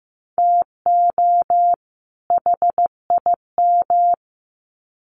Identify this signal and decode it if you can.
Morse code, heard as a keyed tone, about 15 wpm, 700 Hz. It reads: TO HIM
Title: Sets of 2 Words 15wpm